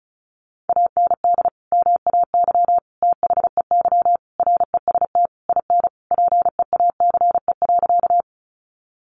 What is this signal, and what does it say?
Morse code, heard as a keyed tone, about 34 wpm, 700 Hz. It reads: AND MAY THEY REST IN PEACE.
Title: Sentences from Top 1500 Words 35wpm